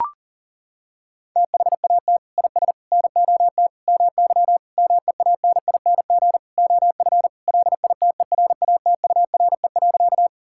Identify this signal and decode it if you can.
Morse code, heard as a keyed tone, about 37 wpm, 700 Hz. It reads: THAT IS NOT MY MEANING OF LITERATURE.
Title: Sentences from Top 1000 Words - Review Part 6 - 40wpm